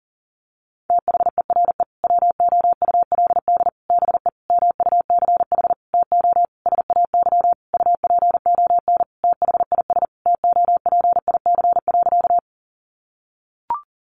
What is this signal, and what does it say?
Morse code, heard as a keyed tone, about 40 wpm, 700 Hz. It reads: THERE WOULD BE MUCH TO SAY UPON THIS TOPIC.
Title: Sentences from Top 1200 Words 40wpm